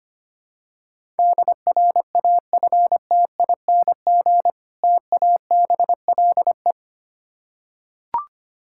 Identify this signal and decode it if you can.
Morse code, heard as a keyed tone, about 25 wpm, 690 Hz. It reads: DRAFTING TABLE
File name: practice / 25wpm / Difficult Words with Hints - 25wpm